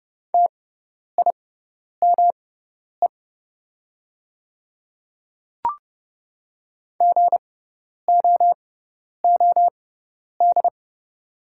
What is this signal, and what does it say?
Morse code, heard as a keyed tone, about 30 wpm, spaced as for 10 wpm, 700 Hz. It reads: TIME GOOD
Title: 6x Character Spacing - Top 200 Words - 30wpm